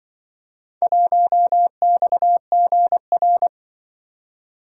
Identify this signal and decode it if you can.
Morse code, heard as a keyed tone, about 25 wpm, 700 Hz. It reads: 1XGR
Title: Speed-Racing - Four Letter-Number Combinations - 20wpm